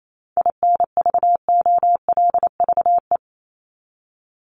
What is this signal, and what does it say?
Morse code, heard as a keyed tone, about 28 wpm, 700 Hz. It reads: INVOLVE
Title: Top 1001-1250 Words 28wpm